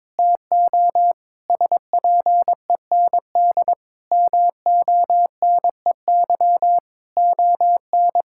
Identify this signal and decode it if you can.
Morse code, heard as a keyed tone, about 22 wpm, 700 Hz. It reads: TO SPEND MONEY ON
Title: Sentences from Top 1000 Words - Review Part 5 - 22wpm